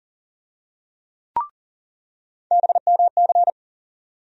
Morse code, heard as a keyed tone, about 40 wpm, 700 Hz. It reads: BMC
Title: USA Call Sign Postfixes - 3 Letters 40wpm